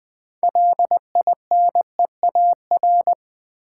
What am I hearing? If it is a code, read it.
Morse code, heard as a keyed tone, about 20 wpm, 700 Hz. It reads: LINEAR
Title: Build A Word - 3 Letters At A Time - 6-Letter Words - 20wpm